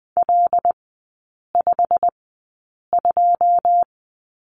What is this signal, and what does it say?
Morse code, heard as a keyed tone, about 20 wpm, spaced as for 8 wpm, 700 Hz. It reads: L52
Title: 178 Lesson 29e - Slash - Three Characters 20wpm